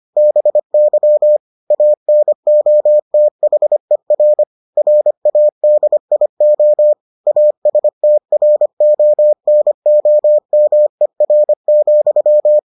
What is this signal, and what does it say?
Morse code, heard as a keyed tone, about 25 wpm, 600 Hz. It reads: BY ANOTHER RADIO ASTRONOMER,